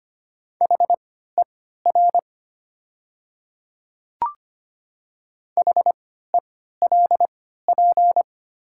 Morse code, heard as a keyed tone, about 25 wpm, spaced as for 14 wpm, 700 Hz. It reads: HER HELP
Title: 3x Character Spacing - Top 500 Words - 25wpm